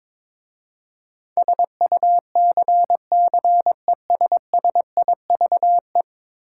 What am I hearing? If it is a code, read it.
Morse code, heard as a keyed tone, about 22 wpm, 700 Hz. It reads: SUCCESSIVE
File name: practice / 22wpm / 10 Letter Words - First 3 Letters Given - 22wpm